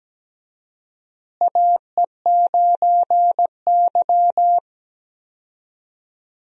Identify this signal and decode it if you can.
Morse code, heard as a keyed tone, about 17 wpm, 700 Hz. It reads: AE9Y
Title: USA Call Signs - 4 Characters - Repeated 2x - 17wpm